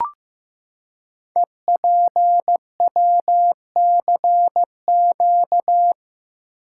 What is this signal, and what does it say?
Morse code, heard as a keyed tone, about 15 wpm, 700 Hz. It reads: EPWCQ
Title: Code Groups - 5 Letter:Numbers - 15wpm